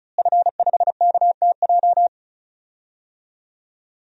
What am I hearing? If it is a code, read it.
Morse code, heard as a keyed tone, about 36 wpm, 700 Hz. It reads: F5KTJ
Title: DXCC-100 Call Signs - 5 Characters - Repeated 3x - 35wpm